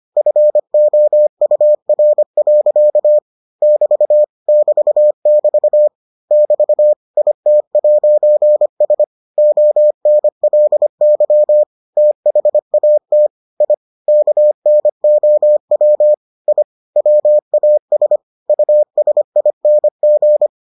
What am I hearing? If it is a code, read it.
Morse code, heard as a keyed tone, about 25 wpm, 600 Hz. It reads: FOUR. <BT> <BT><BT> <BT> IT'S ONLY THAT I KNOW I WAS USING